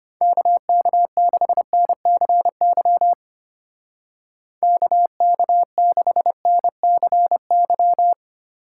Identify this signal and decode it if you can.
Morse code, heard as a keyed tone, about 30 wpm, 700 Hz. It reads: KK6NCY KK6NCY
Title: Speed Racing - USA 2x3 Call Signs 25wpm